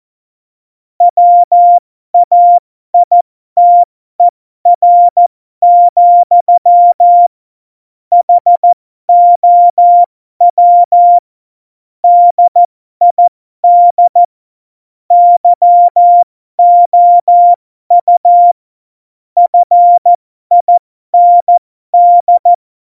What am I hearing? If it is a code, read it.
Morse code, heard as a keyed tone, about 14 wpm, spaced as for 12 wpm, 700 Hz. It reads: WAITER, HOW DID YOU FIND
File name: Morsekurs / Witze